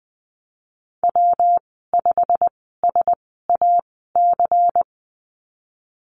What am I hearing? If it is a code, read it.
Morse code, heard as a keyed tone, about 19 wpm, spaced as for 14 wpm, 700 Hz. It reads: W5SAC
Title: 154 Lesson 25g - Letter p - Call Signs 20wpm